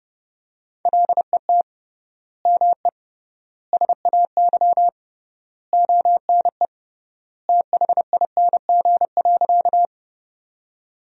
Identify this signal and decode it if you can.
Morse code, heard as a keyed tone, about 32 wpm, 700 Hz. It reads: LET ME SAY ONE THING.
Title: 3x Word-Spacing - Sentences from Top 300 Words - 30wpm